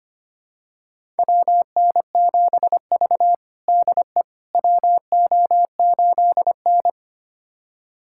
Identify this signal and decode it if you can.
Morse code, heard as a keyed tone, about 25 wpm, 700 Hz. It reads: WN7V DE WO8N
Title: USA Callsign DE Callsign - 4 Characters - 25wpm